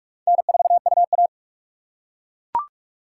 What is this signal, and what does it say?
Morse code, heard as a keyed tone, about 41 wpm, 700 Hz. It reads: N4UA
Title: USA Call Signs - 4 Characters - 3x Character Spacing - 45wpm